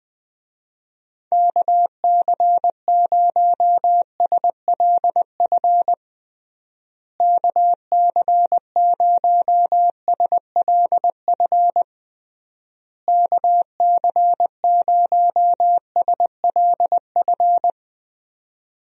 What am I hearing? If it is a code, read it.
Morse code, heard as a keyed tone, about 21 wpm, 700 Hz. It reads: KC0SLF KC0SLF KC0SLF
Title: USA Call Signs - 6 Characters - Repeated 3x - 20wpm